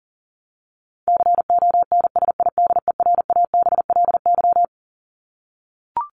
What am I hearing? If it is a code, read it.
Morse code, heard as a keyed tone, about 40 wpm, 700 Hz. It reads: CONSIDERABLY
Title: LY Suffix - Top 100 Words with Base Words 40wpm